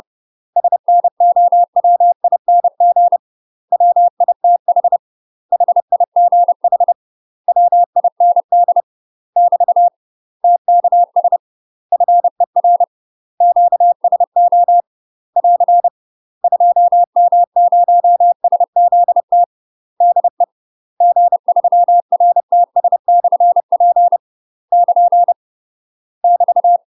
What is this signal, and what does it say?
Morse code, heard as a keyed tone, about 30 wpm, 700 Hz. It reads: SNOWING WITH HIGH WIND <BT> TKS FER QSO <AR> 2M0SZT DE G3RTS/P <KN> <BT>
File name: UK MASTER QSOs TRAINING / 30WPM / UK MASTER QSOs 0029